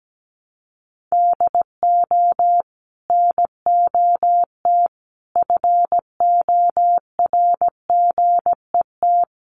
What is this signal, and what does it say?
Morse code, heard as a keyed tone, about 17 wpm, 700 Hz. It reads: DO NOT FORGET
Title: Sentences from Top 1200 Words 17wpm